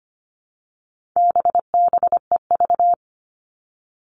Morse code, heard as a keyed tone, about 25 wpm, 700 Hz. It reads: BBEV
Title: All Cut Numbers - 4-Digit 25wpm